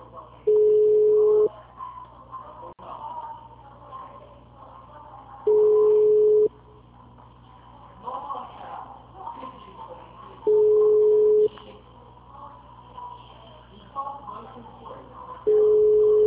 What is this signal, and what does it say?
Ringback tone (Europe, 425 Hz), with speech.